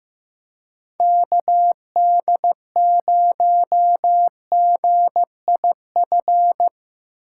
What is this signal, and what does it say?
Morse code, heard as a keyed tone, about 15 wpm, 700 Hz. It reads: KD0GIF
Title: USA Call Signs - 6 Characters - 3x Character Spacing - 15wpm